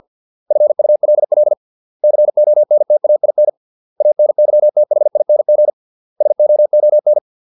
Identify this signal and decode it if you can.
Morse code, heard as a keyed tone, about 50 wpm, 600 Hz. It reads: FULL CONTAIN ANYTHING SOON